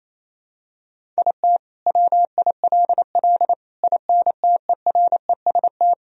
Morse code, heard as a keyed tone, about 27 wpm, 700 Hz. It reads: IT WILL INTEREST
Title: Sentences from Top 400 Words and Abbreviations 28wpm